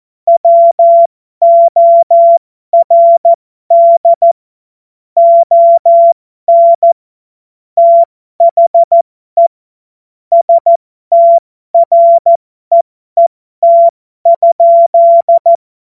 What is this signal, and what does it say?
Morse code, heard as a keyed tone, about 14 wpm, spaced as for 12 wpm, 680 Hz. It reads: WORD ON THE STREET?
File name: Morsekurs / Witze